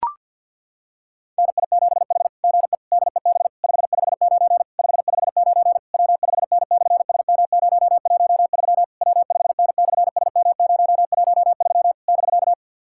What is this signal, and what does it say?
Morse code, heard as a keyed tone, about 51 wpm, 690 Hz. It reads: NI7H GE DEB 559 559 W5N/SM013 W5N/SM013 <BK>
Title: SOTA QSO Practice 50wpm